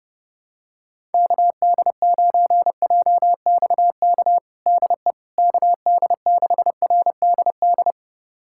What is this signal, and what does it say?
Morse code, heard as a keyed tone, about 32 wpm, 700 Hz. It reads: KD9JXK DE KD6RDD
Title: USA Callsign DE Callsign - 6 Characters - 30wpm